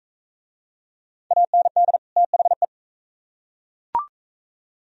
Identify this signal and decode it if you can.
Morse code, heard as a keyed tone, about 39 wpm, 700 Hz. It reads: AND THE